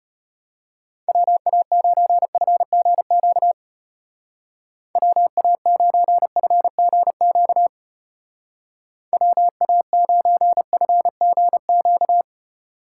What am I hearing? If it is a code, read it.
Morse code, heard as a keyed tone, about 30 wpm, 700 Hz. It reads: WA9FGQ WA9FGQ WA9FGQ
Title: Speed Racing - USA 2x3 Call Signs 25wpm